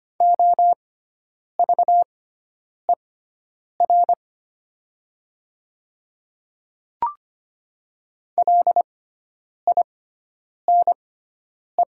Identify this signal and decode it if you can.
Morse code, heard as a keyed tone, about 25 wpm, spaced as for 9 wpm, 700 Hz. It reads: OVER LINE
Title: 6x Character Spacing - Top 300 Words - 25wpm